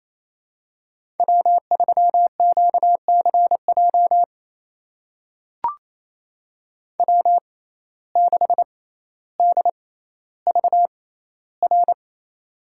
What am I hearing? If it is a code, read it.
Morse code, heard as a keyed tone, about 28 wpm, spaced as for 14 wpm, 700 Hz. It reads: W3QCJ W6DVR
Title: USA Call Signs - 5 Characters - 6x Character Spacing - 28wpm